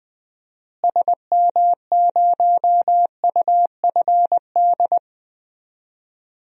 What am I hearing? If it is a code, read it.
Morse code, heard as a keyed tone, about 21 wpm, 700 Hz. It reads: SM0UFD
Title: DXCC-100 Call Signs - 6 Characters - Repeated 3x - 20wpm